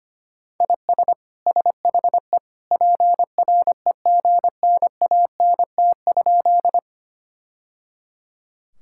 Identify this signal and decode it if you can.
Morse code, heard as a keyed tone, about 25 wpm, 700 Hz. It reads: IS SHE PREGNANT?
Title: Sentences from Top 1000 Words - Review Part 6 - 25wpm